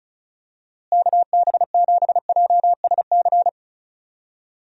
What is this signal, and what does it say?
Morse code, heard as a keyed tone, about 36 wpm, 700 Hz. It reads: KB7JSC
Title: USA Call Signs - 6 Characters - Repeated 2x - 35wpm